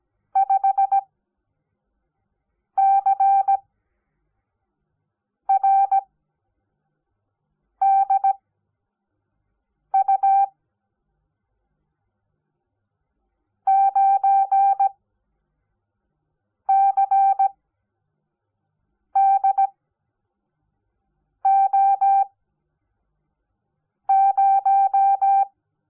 Morse code, heard as a keyed tone, about 17 wpm, spaced as for 4 wpm, 780 Hz. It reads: 5CRDU 9CDO0
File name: Onlinelehrgang / morsen / DJ4UF Morsen05